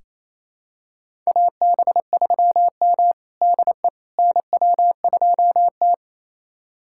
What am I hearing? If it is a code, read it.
Morse code, heard as a keyed tone, about 27 wpm, 690 Hz. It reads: AB3M DE NW2T
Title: USA Callsign DE Callsign - 4 Characters - 28wpm